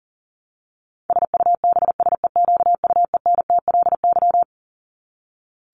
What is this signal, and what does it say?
Morse code, heard as a keyed tone, about 40 wpm, 700 Hz. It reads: SUBSEQUENTLY
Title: LY Suffix - Top 100 Words with Base Words 40wpm